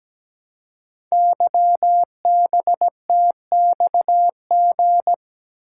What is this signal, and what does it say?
Morse code, heard as a keyed tone, about 17 wpm, 700 Hz. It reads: YBTXG
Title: Code Groups - 5 Letters - 17wpm